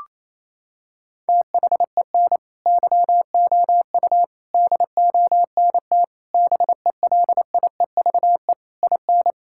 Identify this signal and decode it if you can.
Morse code, heard as a keyed tone, about 28 wpm, 700 Hz. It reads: THEN YOU DONT BELIEVE IN
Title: Sentences from Top 1000 Words - Review Part 6 - 28wpm